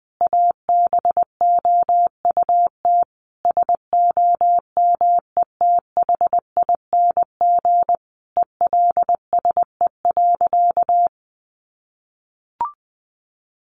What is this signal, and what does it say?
Morse code, heard as a keyed tone, about 20 wpm, 700 Hz. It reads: ABOUT SOMETHING ELSE.
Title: Sentences from Top 800 Words 20wpm